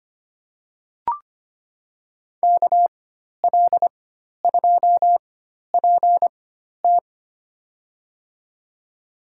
Morse code, heard as a keyed tone, about 25 wpm, spaced as for 12 wpm, 690 Hz. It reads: KL2PT
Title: USA Call Signs - 4x Character Spacing - 25wpm